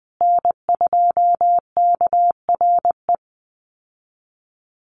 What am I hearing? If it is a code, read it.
Morse code, heard as a keyed tone, about 20 wpm, 700 Hz. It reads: N2KRE
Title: USA Call Signs Repeated 2x 20wpm